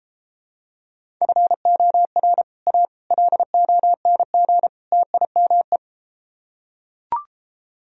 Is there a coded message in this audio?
Yes: Morse code from 1.2 to 5.8 s, heard as a keyed tone, about 32 wpm, 700 Hz. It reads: FOR A LONG TIME